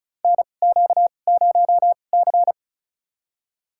Morse code, heard as a keyed tone, about 34 wpm, spaced as for 25 wpm, 700 Hz. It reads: NQ0C
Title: USA Call Signs - 2x Character Spacing - 35wpm